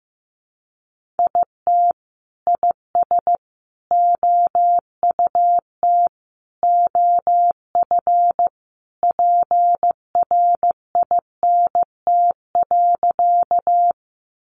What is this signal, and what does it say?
Morse code, heard as a keyed tone, about 15 wpm, 700 Hz. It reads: IT IS OUT OF PRINT.
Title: Sentences from Top 1400 Words 15wpm